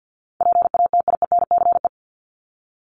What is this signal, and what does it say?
Morse code, heard as a keyed tone, about 50 wpm, 700 Hz. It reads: PATIENCE
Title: Top 1251-1500 Words 50wpm